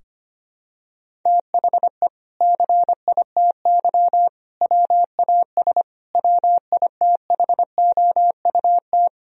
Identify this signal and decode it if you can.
Morse code, heard as a keyed tone, about 25 wpm, 700 Hz. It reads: THE CITY WAS WITHOUT
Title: Sentences from Top 1000 Words - Review Part 3 - 25wpm